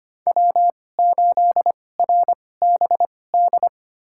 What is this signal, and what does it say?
Morse code, heard as a keyed tone, about 25 wpm, spaced as for 18 wpm, 700 Hz. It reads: W8RBD
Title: USA Call Signs - 5 Characters - 2x Character Spacing - 25wpm